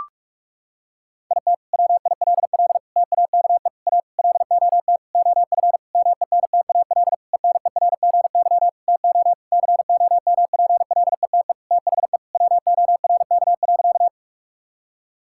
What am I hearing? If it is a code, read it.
Morse code, heard as a keyed tone, about 42 wpm, 700 Hz. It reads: IT WILL TAKE A LOT OF MENTAL ENERGY TO COMPLETE THE WORK.
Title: Sentences from Top 1000 Words - Review Part 6 - 45wpm